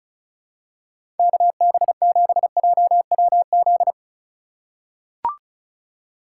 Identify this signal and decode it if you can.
Morse code, heard as a keyed tone, about 34 wpm, 700 Hz. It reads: KB7JWZ
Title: USA Call Signs - 2x Character Spacing - 35wpm